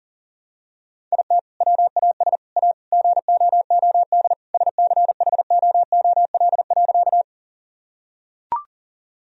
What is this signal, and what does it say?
Morse code, heard as a keyed tone, about 37 wpm, 700 Hz. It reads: IT WAS A GOOD SCHOOL.
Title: Sentences from Top 1000 Words - Review Part 1 - 40wpm